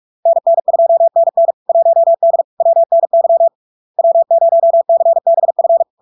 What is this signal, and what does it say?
Morse code, heard as a keyed tone, about 46 wpm, 670 Hz. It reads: NN2NN 1D WNY W0CBF
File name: LARK-FD-2022-Log-in-CW-45wpm-670-Hz